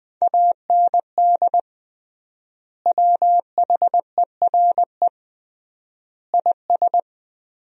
Morse code, heard as a keyed tone, about 20 wpm, 700 Hz. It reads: AND WHERE IS
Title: 3x Word-Spacing - Sentences from Top 200 Words - 20wpm